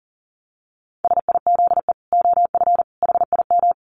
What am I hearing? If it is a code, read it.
Morse code, heard as a keyed tone, about 40 wpm, 700 Hz. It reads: SIZE OF HIM
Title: Dit Challenge 40wpm